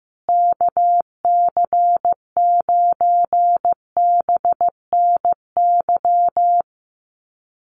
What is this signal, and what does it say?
Morse code, heard as a keyed tone, about 15 wpm, 700 Hz. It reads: KC9BNY
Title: USA Call Signs Repeated 2x 15wpm